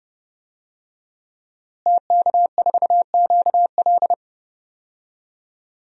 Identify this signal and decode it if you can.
Morse code, heard as a keyed tone, about 30 wpm, 700 Hz. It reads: TK4QL
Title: DXCC-100 Call Signs - 5 Characters - Repeated 3x - 30wpm